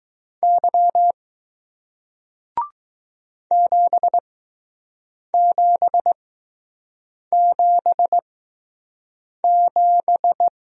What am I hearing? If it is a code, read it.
Morse code, heard as a keyed tone, about 20 wpm, spaced as for 6 wpm, 700 Hz. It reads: Y 7777